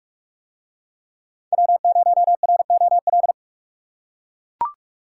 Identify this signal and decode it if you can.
Morse code, heard as a keyed tone, about 43 wpm, 700 Hz. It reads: W0ROL